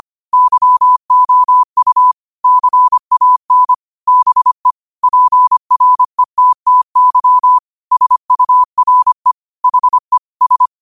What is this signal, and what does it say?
Morse code, heard as a keyed tone, about 25 wpm, 1000 Hz. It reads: YOU CAN BE PRETTY SURE HE S